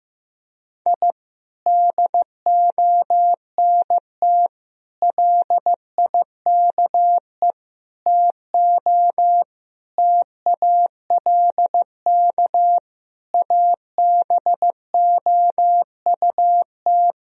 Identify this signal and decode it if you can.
Morse code, heard as a keyed tone, about 15 wpm, 700 Hz. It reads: I DONT LIKE TO TALK ABOUT
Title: Sentences from Top 1000 Words - Review Part 1 - 15wpm